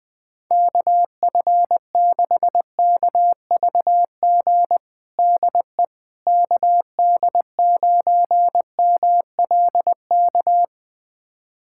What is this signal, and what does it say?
Morse code, heard as a keyed tone, about 20 wpm, 700 Hz. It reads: KF6KVG DE KD9MLK
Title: USA Callsign DE Callsign 20wpm